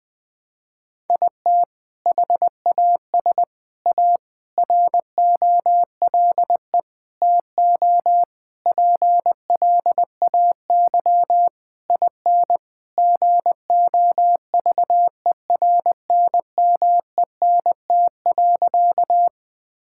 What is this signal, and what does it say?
Morse code, heard as a keyed tone, about 20 wpm, 700 Hz. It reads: IT HAS A ROLE TO PLAY IN GOVERNMENT.
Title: Sentences from Top 1000 Words - Review Part 6 - 20wpm